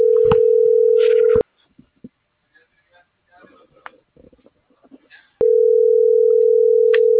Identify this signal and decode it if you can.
Ringback tone (North America), 440 + 480 Hz.